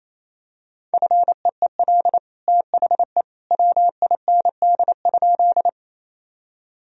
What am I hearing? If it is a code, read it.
Morse code, heard as a keyed tone, about 28 wpm, 700 Hz. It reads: FEEL THE WIND?
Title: Sentences from Top 1000 Words - Review Part 2 - 28wpm